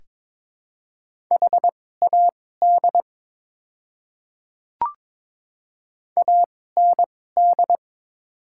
Morse code, heard as a keyed tone, about 22 wpm, spaced as for 16 wpm, 700 Hz. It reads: HAD AND
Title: 2x Character Spacing - Top 100 Words - 22wpm